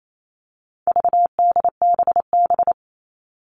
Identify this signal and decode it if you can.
Morse code, heard as a keyed tone, about 28 wpm, 700 Hz. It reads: VDBB
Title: All Cut Numbers - 4-Digit 28wpm